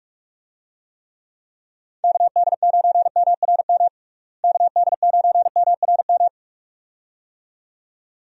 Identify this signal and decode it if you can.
Morse code, heard as a keyed tone, about 41 wpm, 700 Hz. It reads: KD9MRM KD9MRM
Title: USA Call Signs - 6 Characters - Repeated 2x - 45wpm